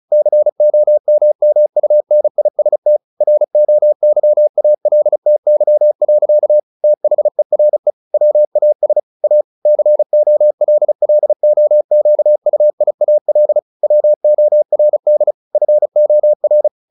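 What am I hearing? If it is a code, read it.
Morse code, heard as a keyed tone, about 34 wpm, 600 Hz. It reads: COMMUNIST ROYALTY. THERE WAS A COLLOQUIAL WORD FOR
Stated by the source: Morse practice at 35 wpm from CW med Gnister, sending Ken Follett's novel Never in English.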